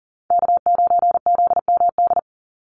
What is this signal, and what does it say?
Morse code, heard as a keyed tone, about 40 wpm, 700 Hz. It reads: K9ZMD
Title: NA SOTA Chaser Callsigns 40wpm